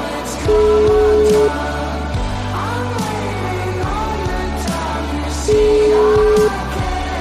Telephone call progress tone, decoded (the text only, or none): ringback tone